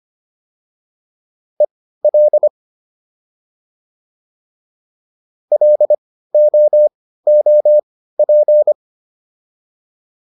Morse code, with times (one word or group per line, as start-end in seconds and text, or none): 1.6-2.5 EL
5.5-8.7 LOOP